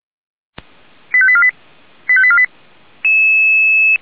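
Five-tone selective call (ZVEI), heard as quick runs of five tones, tones of about 70 ms. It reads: 85848 85848
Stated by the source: ZVEI five-tone selective call received on an SDR.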